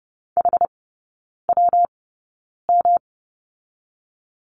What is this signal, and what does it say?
Morse code, heard as a keyed tone, about 30 wpm, spaced as for 9 wpm, 700 Hz. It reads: HWM
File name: Lesson 19e - Letter w - Three Characters 30wpm